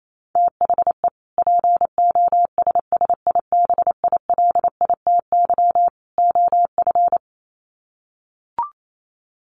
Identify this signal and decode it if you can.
Morse code, heard as a keyed tone, about 28 wpm, 700 Hz. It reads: THE POSSIBILITY OF
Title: Sets of 3 Words 28wpm